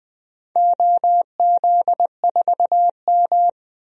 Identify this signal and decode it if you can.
Morse code, heard as a keyed tone, about 21 wpm, 700 Hz. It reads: OZ4M
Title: DXCC-100 Call Signs - 4 Characters - Repeated 2x - 20wpm